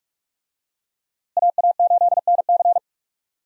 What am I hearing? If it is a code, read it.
Morse code, heard as a keyed tone, about 45 wpm, 700 Hz. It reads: AA8NC
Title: USA Call Signs - 5 Characters - Repeated 3x - 45wpm